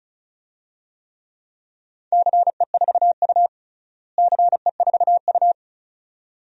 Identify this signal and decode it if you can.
Morse code, heard as a keyed tone, about 34 wpm, 700 Hz. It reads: CE4U CE4U
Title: DXCC-100 Call Signs - 4 Characters - Repeated 2x - 35wpm